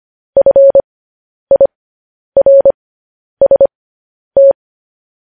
Morse code, heard as a keyed tone, about 25 wpm, spaced as for 10 wpm, 550 Hz. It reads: FIRST
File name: podcast / News headlines-1x-25x10